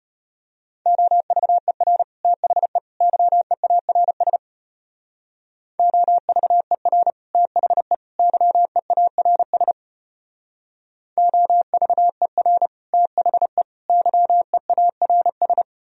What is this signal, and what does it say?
Morse code, heard as a keyed tone, about 32 wpm, 700 Hz. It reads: OVER THE YEARS OVER THE YEARS OVER THE YEARS